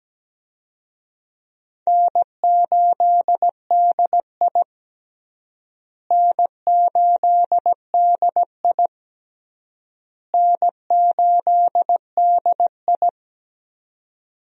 Morse code, heard as a keyed tone, about 17 wpm, 700 Hz. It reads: N8DI N8DI N8DI
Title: USA Call Signs - 4 Characters - Repeated 3x - 17wpm